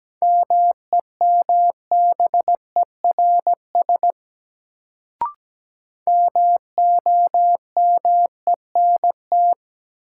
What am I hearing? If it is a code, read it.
Morse code, heard as a keyed tone, about 17 wpm, 700 Hz. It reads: MEMBERS MOMENT
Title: Sets of 3 Words - New Word Drill - 17wpm